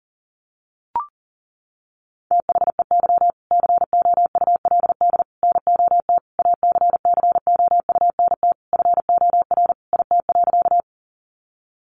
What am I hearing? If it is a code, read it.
Morse code, heard as a keyed tone, about 40 wpm, 700 Hz. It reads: THEY COULD NOT ACCOUNT FOR IT.
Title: Sentences from Top 900 Words 40wpm